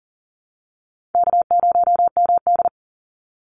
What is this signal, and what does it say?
Morse code, heard as a keyed tone, about 40 wpm, 700 Hz. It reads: K0MD
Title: Easy Call Signs - 40wpm